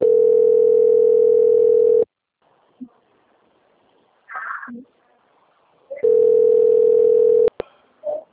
Ringback tone (North America), 440 + 480 Hz.
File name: out- - - - - .3953